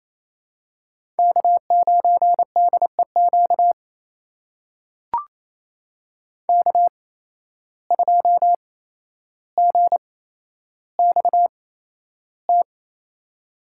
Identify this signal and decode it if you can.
Morse code, heard as a keyed tone, about 27 wpm, spaced as for 12 wpm, 700 Hz. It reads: K9DEQ K2GXT
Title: USA Call Signs - 5 Characters - 8x Character Spacing - 28wpm